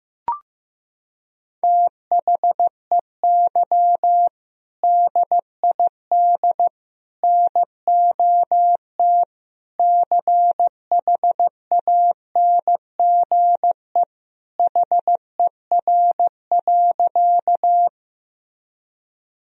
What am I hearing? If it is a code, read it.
Morse code, heard as a keyed tone, about 15 wpm, 700 Hz. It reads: THEY DID NOT CHANGE HER.
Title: Sentences from Top 1000 Words - Review Part 1 - 15wpm